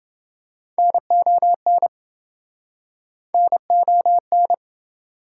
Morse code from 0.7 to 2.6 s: NON

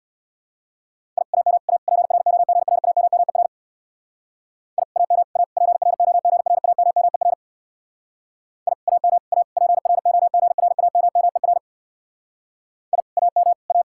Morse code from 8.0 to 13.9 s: I AM A PROGRAMMER I AM A

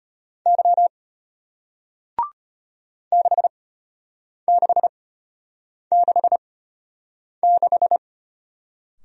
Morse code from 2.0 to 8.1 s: 6666

